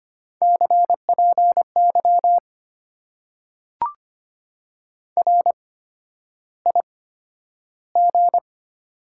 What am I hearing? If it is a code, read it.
Morse code, heard as a keyed tone, about 25 wpm, spaced as for 11 wpm, 700 Hz. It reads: CPY RIG
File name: Top 90 QSO Elements - 8x Character Spacing - 25wpm